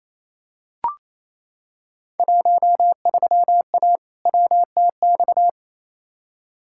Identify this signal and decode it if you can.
Morse code, heard as a keyed tone, about 28 wpm, 700 Hz. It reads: 13A WTX
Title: Field Day Exchanges Encore - 28wpm